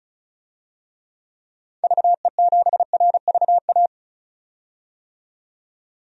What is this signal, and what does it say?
Morse code, heard as a keyed tone, about 34 wpm, 700 Hz. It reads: VE7RVA
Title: DXCC-100 Call Signs - 6 Characters - 35wpm